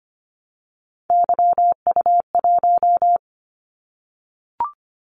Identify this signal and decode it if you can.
Morse code, heard as a keyed tone, about 25 wpm, 700 Hz. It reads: YU1
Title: Callsign Prefixes for Top 25 DXCC - 25wpm